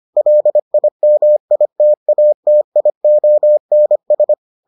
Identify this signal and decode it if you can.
Morse code, heard as a keyed tone, about 25 wpm, 600 Hz. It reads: LIMITATIONS